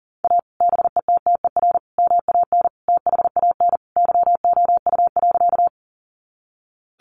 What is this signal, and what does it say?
Morse code, heard as a keyed tone, about 40 wpm, 700 Hz. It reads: A BETTER MAN THAN YOU.
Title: Sentences from Top 400 Words 40wpm